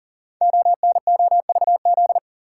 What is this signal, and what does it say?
Morse code, heard as a keyed tone, about 41 wpm, 700 Hz. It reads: ONOVZ
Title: Code Groups - 5 Letters - 40wpm